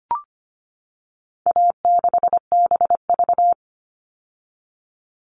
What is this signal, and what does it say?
Morse code, heard as a keyed tone, about 25 wpm, 700 Hz. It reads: A6BV
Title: All Cut Numbers - 4-Digit 25wpm